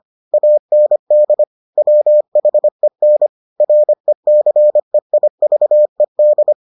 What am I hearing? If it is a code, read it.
Morse code, heard as a keyed tone, about 25 wpm, 600 Hz. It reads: AND WHEN RECEIVED